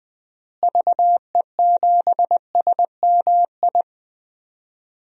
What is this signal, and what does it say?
Morse code, heard as a keyed tone, about 20 wpm, 700 Hz. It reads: VE7SMI
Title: DXCC-100 Call Signs - 6 Characters - Repeated 2x - 20wpm